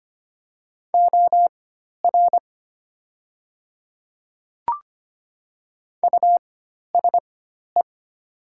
Morse code, heard as a keyed tone, about 25 wpm, spaced as for 12 wpm, 700 Hz. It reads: OR USE
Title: 4x Character Spacing - Top 200 Words - 25wpm